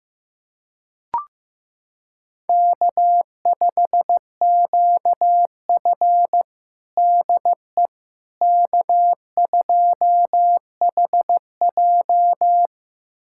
Morse code, heard as a keyed tone, about 15 wpm, 700 Hz. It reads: K5QF DE K2HJ
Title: USA Callsign DE Callsign - 4 Characters - 15wpm